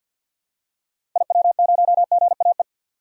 Morse code, heard as a keyed tone, about 49 wpm, 700 Hz. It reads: IW0GAE
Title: DXCC-100 Call Signs - 6 Characters - 50wpm